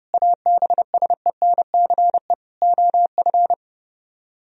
Morse code, heard as a keyed tone, about 30 wpm, 700 Hz. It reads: ABSENCE OF